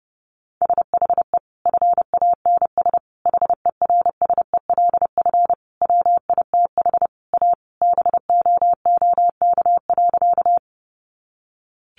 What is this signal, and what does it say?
Morse code, heard as a keyed tone, about 30 wpm, 700 Hz. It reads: SHE FANS HERSELF WITH A BOOK.
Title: Additional Easy Phrases 30wpm